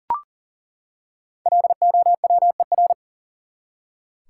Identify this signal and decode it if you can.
Morse code, heard as a keyed tone, about 37 wpm, 700 Hz. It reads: LOWER
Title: Five Letter Words - 40wpm